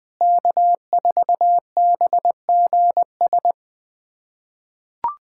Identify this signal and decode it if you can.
Morse code, heard as a keyed tone, about 21 wpm, 700 Hz. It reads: K4BGS
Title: USA Callsign DE Callsign - 5 Characters - 20wpm